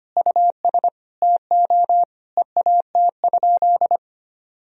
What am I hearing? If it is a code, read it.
Morse code, heard as a keyed tone, about 25 wpm, 700 Hz. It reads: US TO EAT?
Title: Sentences from Top 1000 Words - Review Part 3 - 25wpm